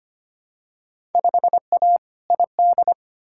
Morse code, heard as a keyed tone, about 25 wpm, 700 Hz. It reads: 5A ID
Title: Field Day Exchanges - 25wpm